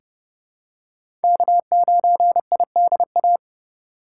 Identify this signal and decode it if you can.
Morse code, heard as a keyed tone, about 30 wpm, 700 Hz. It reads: K9IDA
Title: USA Call Signs - 5 Characters - 2x Character Spacing - 30wpm